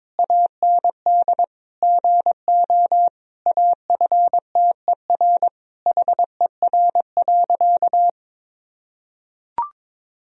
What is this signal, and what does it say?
Morse code, heard as a keyed tone, about 22 wpm, 700 Hz. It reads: AND GO AFTER HER.
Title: Sentences from Top 1000 Words - Review Part 3 - 22wpm